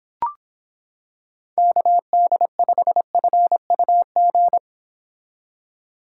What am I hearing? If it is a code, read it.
Morse code, heard as a keyed tone, about 26 wpm, 700 Hz. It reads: KD5FUG